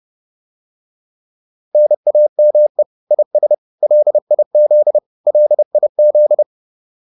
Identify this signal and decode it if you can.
Morse code, heard as a keyed tone, about 30 wpm, 600 Hz. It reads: NAME IS LIZ LIZ